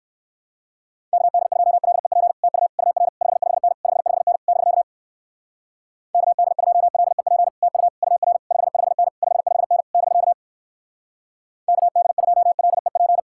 Morse code, heard as a keyed tone, about 59 wpm, 700 Hz. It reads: KD2LEP TU UR 55N 55N <BK> KD2LEP TU UR 55N 55N <BK> KD2LEP